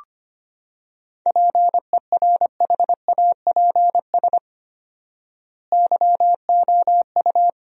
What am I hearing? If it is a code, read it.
Morse code, heard as a keyed tone, about 25 wpm, 700 Hz. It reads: PERHAPS YOU
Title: 4x Word-Spacing - Sentences from Top 500 Words - 25wpm